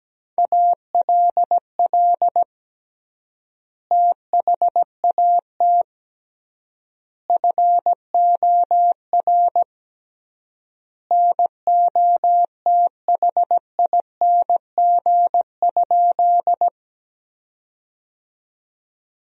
Morse code, heard as a keyed tone, about 17 wpm, 700 Hz. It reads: ALL THAT FOR NOTHING?
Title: 3x Word-Spacing - Sentences from Top 500 Words - 17wpm